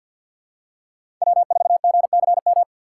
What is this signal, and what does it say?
Morse code, heard as a keyed tone, about 51 wpm, 700 Hz. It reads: W4GCM
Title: USA Call Signs - 5 Characters - 2x Character Spacing - 50wpm